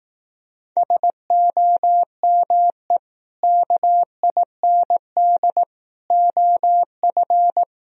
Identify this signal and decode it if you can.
Morse code, heard as a keyed tone, about 18 wpm, 700 Hz. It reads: SOME KIND OF
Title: Speed-Racing - Sets of 3 Words 15wpm